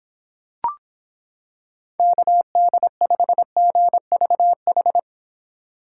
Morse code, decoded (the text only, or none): KD5GVH